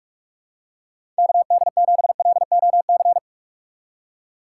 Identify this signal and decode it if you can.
Morse code, heard as a keyed tone, about 46 wpm, 700 Hz. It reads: KD7LOC
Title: USA Call Signs - 6 Characters - 2x Character Spacing - 45wpm